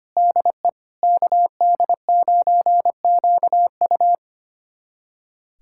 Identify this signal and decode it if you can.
Morse code, heard as a keyed tone, about 25 wpm, 700 Hz. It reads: DE KD9QU
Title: USA Callsign DE Callsign 25wpm